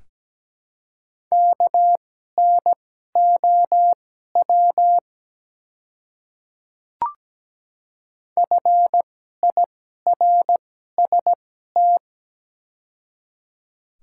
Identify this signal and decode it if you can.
Morse code, heard as a keyed tone, about 17 wpm, spaced as for 12 wpm, 700 Hz. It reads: KNOW FIRST
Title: 2x Character Spacing - Top 100 Words - 17wpm